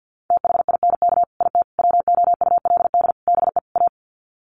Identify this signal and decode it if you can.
Morse code, heard as a keyed tone, about 50 wpm, 700 Hz. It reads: THINK IT WOULD BE A
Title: Sets of 5 Words Encore2 50wpm